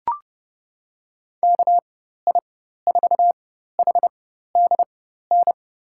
Morse code, heard as a keyed tone, about 32 wpm, spaced as for 14 wpm, 700 Hz. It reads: KI4HDN
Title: USA Call Signs - 6 Characters - 4x Character Spacing - 30wpm